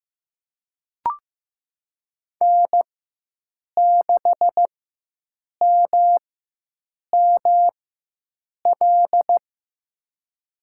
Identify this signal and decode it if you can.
Morse code, heard as a keyed tone, about 16 wpm, spaced as for 7 wpm, 700 Hz. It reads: N6MML
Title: USA Call Signs - 5 Characters - 4x Character Spacing - 15wpm